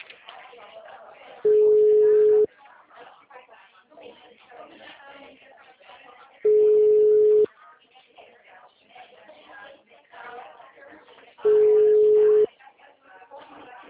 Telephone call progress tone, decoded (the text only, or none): ringback tone